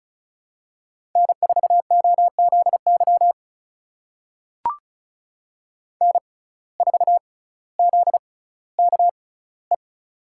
Morse code, heard as a keyed tone, about 36 wpm, spaced as for 18 wpm, 700 Hz. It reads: N4OZY N4ZKE
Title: USA Call Signs - 5 Characters - 6x Character Spacing - 35wpm